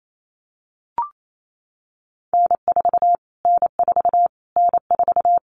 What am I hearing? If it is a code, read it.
Morse code, heard as a keyed tone, about 28 wpm, 700 Hz. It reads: N4 N4 N4
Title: Callsign Prefixes for Top 25 DXCC - Repeated 3x - 28wpm